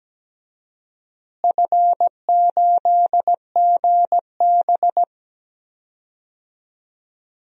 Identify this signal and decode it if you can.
Morse code, heard as a keyed tone, about 17 wpm, 700 Hz. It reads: F8GB
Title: DXCC-100 Call Signs - 4 Characters - Repeated 3x - 17wpm